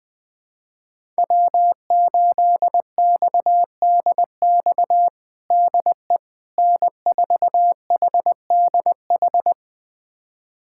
Morse code, heard as a keyed tone, about 20 wpm, 700 Hz. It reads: W8XDX DE N4HDH
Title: USA Callsign DE Callsign - 5 Characters - 20wpm